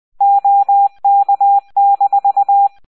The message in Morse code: OK-